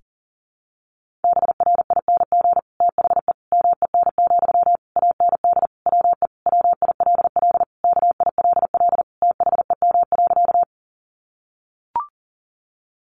Morse code, heard as a keyed tone, about 40 wpm, 700 Hz. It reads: BRING THE MEN, AND WE WILL KILL THEM.
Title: Sentences from Top 1400 Words 40wpm